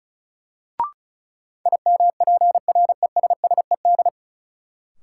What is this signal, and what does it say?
Morse code, heard as a keyed tone, about 36 wpm, 700 Hz. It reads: IMPRESSED
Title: Sentences from Top 1600 Words - New Word Drill - 35wpm